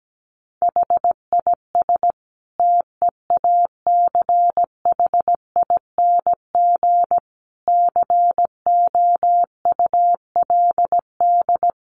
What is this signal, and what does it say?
Morse code, heard as a keyed tone, about 17 wpm, 700 Hz. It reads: HIS TEACHING COULD
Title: Sentences from Top 1100 Words 17wpm